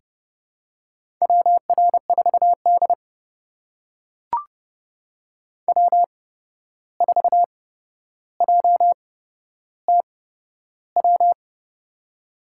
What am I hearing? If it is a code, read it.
Morse code, heard as a keyed tone, about 30 wpm, spaced as for 12 wpm, 700 Hz. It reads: WR4D W4JTW
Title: USA Call Signs - 8x Character Spacing - 30wpm